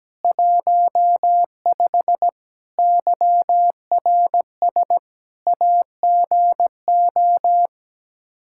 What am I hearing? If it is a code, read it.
Morse code, heard as a keyed tone, about 17 wpm, 700 Hz. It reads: 15 YRS AGO
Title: K1USN SST QSO Practice 17wpm